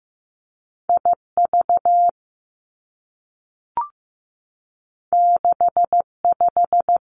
Morse code, heard as a keyed tone, about 15 wpm, 700 Hz. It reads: I V 6 5
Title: Two Challenging Letter-Number Combinations 15wpm